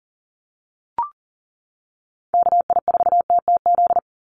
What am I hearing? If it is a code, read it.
Morse code, heard as a keyed tone, about 40 wpm, 700 Hz. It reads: KI4TTZ
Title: USA Call Signs 40wpm